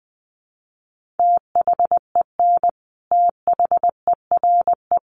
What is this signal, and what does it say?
Morse code, heard as a keyed tone, about 19 wpm, 700 Hz. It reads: THEN THERE
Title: Sentences from Top 1400 Words 20wpm